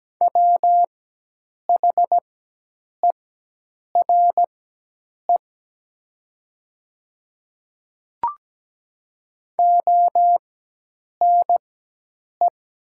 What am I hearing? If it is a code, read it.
Morse code, heard as a keyed tone, about 17 wpm, spaced as for 8 wpm, 700 Hz. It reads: WHERE ONE
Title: 4x Character Spacing - Top 200 Words - 17wpm